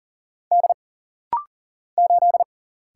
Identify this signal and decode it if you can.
Morse code, heard as a keyed tone, about 43 wpm, 700 Hz. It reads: D 8
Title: Single Letter-Number - Rapid-Fire 40wpm